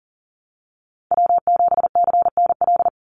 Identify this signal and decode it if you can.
Morse code, heard as a keyed tone, about 40 wpm, 700 Hz. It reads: W7CNL
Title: NA SOTA Chaser Callsigns 40wpm